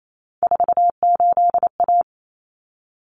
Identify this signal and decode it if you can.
Morse code, heard as a keyed tone, about 28 wpm, 700 Hz. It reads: 48A
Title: Common Cut Numbers - 3-Digit 28wpm